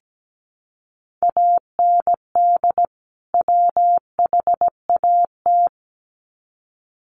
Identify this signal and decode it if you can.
Morse code, heard as a keyed tone, about 17 wpm, 700 Hz. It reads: AND WHAT
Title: Sets of 2 Words 17wpm